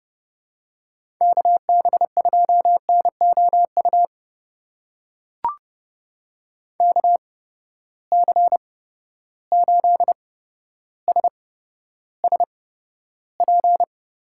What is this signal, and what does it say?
Morse code, heard as a keyed tone, about 30 wpm, spaced as for 13 wpm, 700 Hz. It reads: KB2NOU KC8SSP
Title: USA Call Signs - 6 Characters - 8x Character Spacing - 30wpm